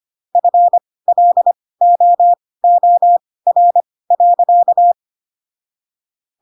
Morse code, heard as a keyed tone, about 25 wpm, spaced as for 18 wpm, 700 Hz. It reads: FLOOR.